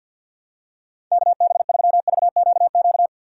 Morse code, heard as a keyed tone, about 51 wpm, 700 Hz. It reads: KB3VQQ
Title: USA Call Signs - 6 Characters - 4x Character Spacing - 50wpm